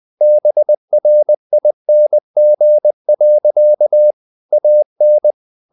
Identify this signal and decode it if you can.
Morse code, heard as a keyed tone, about 20 wpm, 600 Hz. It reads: BRING. AN